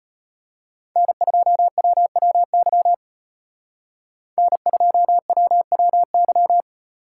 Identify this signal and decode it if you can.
Morse code, heard as a keyed tone, about 32 wpm, 700 Hz. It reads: N2WWY N2WWY